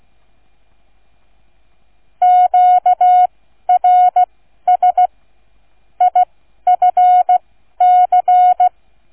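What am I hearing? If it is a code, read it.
Morse code, heard as a keyed tone, about 16 wpm, spaced as for 12 wpm, 730 Hz. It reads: QRS IFC
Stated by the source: Morse code receive practice, flight number, from ZL3TK.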